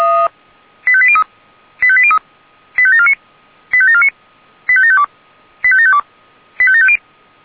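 Five-tone selective call (ZVEI), heard as quick runs of five tones, tones of about 70 ms. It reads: 85803 85803 85749 85749 85752 85752 85750
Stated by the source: ZVEI five-tone call received by SDR.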